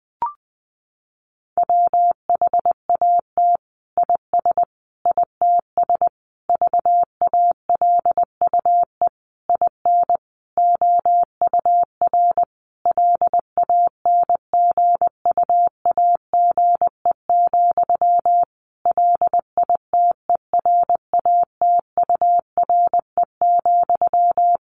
Morse code, heard as a keyed tone, about 20 wpm, 700 Hz. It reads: WHAT IS ITS VALUE IN OUR LANGUAGE, LITERATURE,
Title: Sentences from Top 600 Words 20wpm